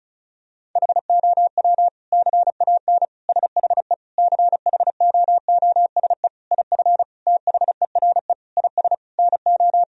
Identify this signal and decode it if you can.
Morse code, heard as a keyed tone, about 36 wpm, 700 Hz. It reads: HOW CAN SHE CHOOSE IF THERE IS NO
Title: Sentences from Top 1000 Words - Review Part 4 - 35wpm